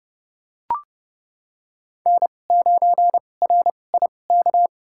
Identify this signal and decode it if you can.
Morse code, heard as a keyed tone, about 30 wpm, spaced as for 22 wpm, 700 Hz. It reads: N9RIK
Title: USA Call Signs - 5 Characters - 2x Character Spacing - 30wpm